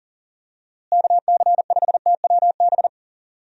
Morse code, heard as a keyed tone, about 37 wpm, 700 Hz. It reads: KC5TWB
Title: USA Call Signs - 6 Characters - Repeated 2x - 40wpm